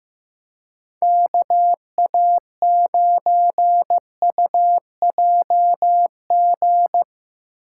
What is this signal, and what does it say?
Morse code, heard as a keyed tone, about 15 wpm, 700 Hz. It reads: KA9UJG
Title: USA Call Signs - 6 Characters - 2x Character Spacing - 15wpm